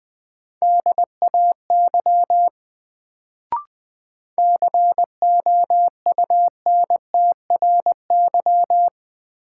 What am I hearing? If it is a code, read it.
Morse code, heard as a keyed tone, about 20 wpm, 700 Hz. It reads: DAY COUNTRY
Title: Sets of 2 Words - New Word Drill - 20wpm